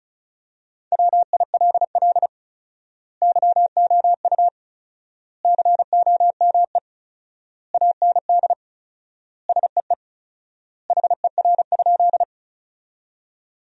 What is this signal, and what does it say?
Morse code, heard as a keyed tone, about 36 wpm, 700 Hz. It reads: WILL YOU COME AND SEE HER?
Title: 4x Word-Spacing - Sentences from Top 100 Words - 35wpm